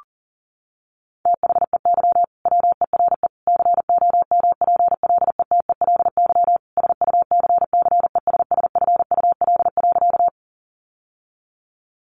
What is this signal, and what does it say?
Morse code, heard as a keyed tone, about 40 wpm, 700 Hz. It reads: THEY WERE COMPLETELY SUCCESSFUL.
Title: Sentences from Top 1100 Words 40wpm